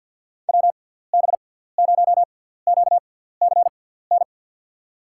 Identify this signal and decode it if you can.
Morse code, heard as a keyed tone, about 52 wpm, 700 Hz. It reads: W B 0 Q C N
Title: USA Call Signs - 6 Characters - 6x Character Spacing - 50wpm